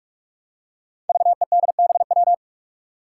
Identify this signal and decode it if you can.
Morse code, heard as a keyed tone, about 46 wpm, 700 Hz. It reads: VEDBW
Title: Five Challenging Letter-Number Combinations 45wpm